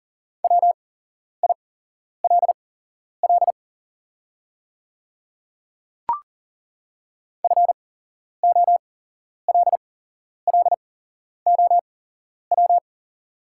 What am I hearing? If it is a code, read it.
Morse code, heard as a keyed tone, about 40 wpm, spaced as for 11 wpm, 700 Hz. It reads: WILL FOLLOW
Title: 8x Character Spacing - Top 200 Words - 40wpm